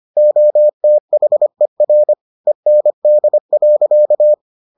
Morse code, heard as a keyed tone, about 25 wpm, 600 Hz. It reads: OTHER END.